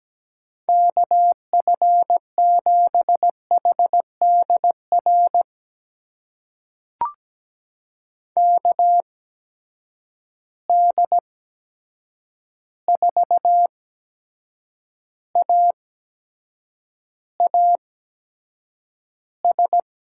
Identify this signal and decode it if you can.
Morse code, heard as a keyed tone, about 17 wpm, spaced as for 7 wpm, 700 Hz. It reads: KF7HDR KD4AAS
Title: USA Call Signs - 6 Characters - 8x Character Spacing - 17wpm